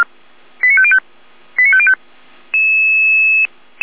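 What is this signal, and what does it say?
Five-tone selective call (ZVEI), heard as quick runs of five tones, tones of about 70 ms. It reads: 89505 89505
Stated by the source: ZVEI five-tone selective call received on an SDR.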